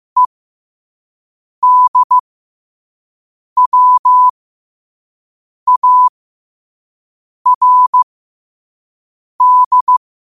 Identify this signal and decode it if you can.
Morse code, heard as a keyed tone, about 15 wpm, spaced as for 5 wpm, 1000 Hz. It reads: EDWARD
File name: qotdfiles / 5wpm